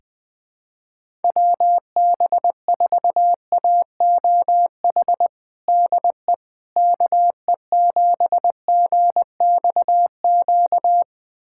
Morse code, heard as a keyed tone, about 20 wpm, 700 Hz. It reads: WB4AOH DE KE7GXQ
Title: USA Callsign DE Callsign 20wpm